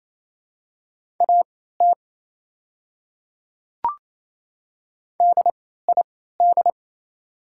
Morse code, heard as a keyed tone, about 28 wpm, spaced as for 16 wpm, 700 Hz. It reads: AT DID
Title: 3x Character Spacing - Top 100 Words - 28wpm